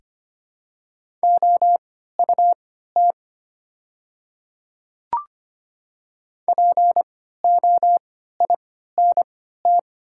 Morse code, heard as a keyed tone, about 25 wpm, spaced as for 14 wpm, 700 Hz. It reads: OUT POINT
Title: 3x Character Spacing - Top 200 Words - 25wpm